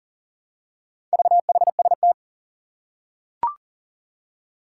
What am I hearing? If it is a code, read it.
Morse code, heard as a keyed tone, about 38 wpm, 700 Hz. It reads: VHST
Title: Four Challenging Letter-Number Combinations 40wpm